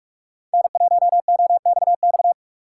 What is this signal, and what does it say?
Morse code, heard as a keyed tone, about 44 wpm, 700 Hz. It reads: N1OXX